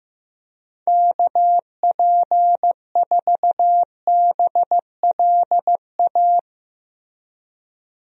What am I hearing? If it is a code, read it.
Morse code, heard as a keyed tone, about 16 wpm, 700 Hz. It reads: KP4BLA
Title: DXCC-100 Call Signs - 6 Characters - Repeated 3x - 15wpm